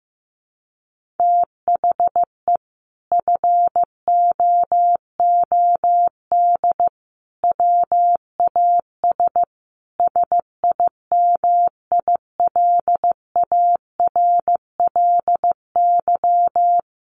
Morse code, heard as a keyed tone, about 15 wpm, 700 Hz. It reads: THE FOOD WAS SIMILARLY
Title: Sentences from Top 1400 Words 15wpm